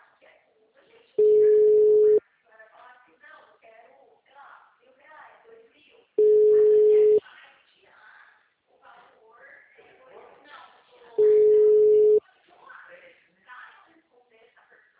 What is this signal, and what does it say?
Ringback tone (Europe, 425 Hz), with speech.
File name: out- -2006- - - .355